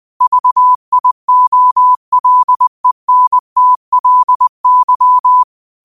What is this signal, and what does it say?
Morse code, heard as a keyed tone, about 20 wpm, 1000 Hz. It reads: VIOLENTLY
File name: 20wpm